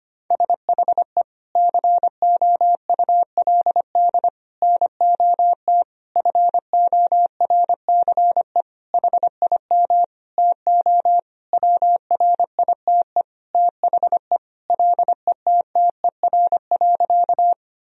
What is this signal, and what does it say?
Morse code, heard as a keyed tone, about 25 wpm, 700 Hz. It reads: SHE COULD NOT FORCE HIM TO WRITE THE LETTER.
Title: Sentences from Top 1000 Words - Review Part 3 - 25wpm